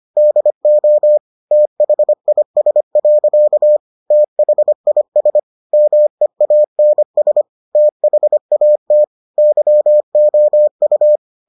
Morse code, heard as a keyed tone, about 25 wpm, 600 Hz. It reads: DO THIS. THIS MEANS THAT YOU